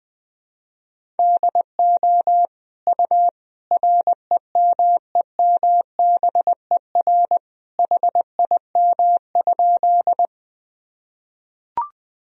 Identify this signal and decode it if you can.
Morse code, heard as a keyed tone, about 20 wpm, 700 Hz. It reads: DO U REMEMBER HIM?
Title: Sentences from Top 400 Words and Abbreviations 20wpm